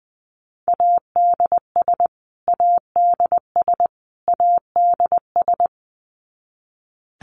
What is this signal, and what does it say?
Morse code, heard as a keyed tone, about 20 wpm, 700 Hz. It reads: ADS ADS ADS
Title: Abbreviations Repeated 3x - 20wpm